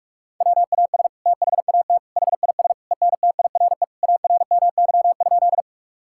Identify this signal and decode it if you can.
Morse code, heard as a keyed tone, about 47 wpm, 700 Hz. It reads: WAS THAT HIS ENTIRE ARMY?
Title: Sentences from Top 1000 Words - Review Part 5 - 45wpm